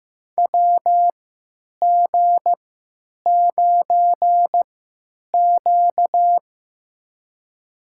Morse code, heard as a keyed tone, about 15 wpm, spaced as for 9 wpm, 700 Hz. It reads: WG9Q
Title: USA Call Signs - 3x Character Spacing - 15wpm